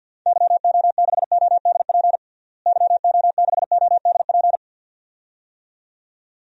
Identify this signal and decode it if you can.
Morse code, heard as a keyed tone, about 50 wpm, 700 Hz. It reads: YO6ODP YO6ODP
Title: DXCC-100 Call Signs - 6 Characters - Repeated 2x - 50wpm